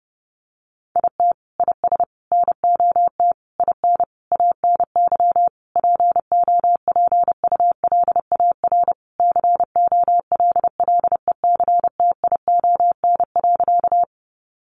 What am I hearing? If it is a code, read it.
Morse code, heard as a keyed tone, about 30 wpm, 700 Hz. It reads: IT IS NOT IN ANY POPULAR COLLECTION.
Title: Sentences from Top 1200 Words 30wpm